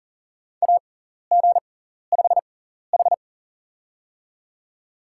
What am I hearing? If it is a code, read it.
Morse code, heard as a keyed tone, about 37 wpm, spaced as for 14 wpm, 700 Hz. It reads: AG5H
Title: USA Call Signs - 4 Characters - 6x Character Spacing - 40wpm